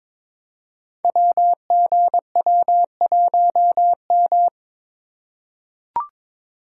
Morse code, heard as a keyed tone, about 22 wpm, 700 Hz. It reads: WGW1M
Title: Code Groups - 5 Letter:Numbers - 22wpm